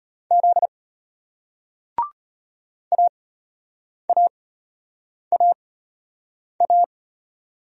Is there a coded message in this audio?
Yes: Morse code from 0.3 to 6.8 s, heard as a keyed tone, about 34 wpm, spaced as for 8 wpm, 700 Hz. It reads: Z AAAA